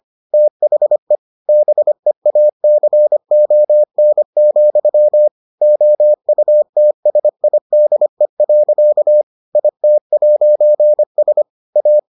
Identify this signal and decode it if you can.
Morse code, heard as a keyed tone, about 25 wpm, 600 Hz. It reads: THE BEACON, OUTSIDE. IT'S A